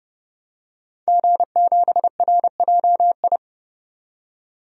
Morse code, heard as a keyed tone, about 30 wpm, 700 Hz. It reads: G7RJI
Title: DXCC-100 Call Signs - 5 Characters - 30wpm